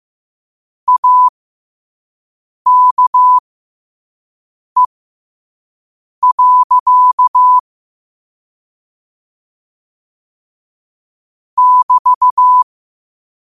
Morse code, heard as a keyed tone, about 15 wpm, spaced as for 5 wpm, 1000 Hz. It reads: AKE. <BT>